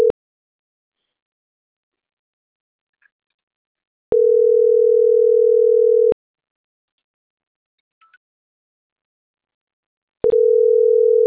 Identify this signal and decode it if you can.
Ringback tone (North America), 440 + 480 Hz.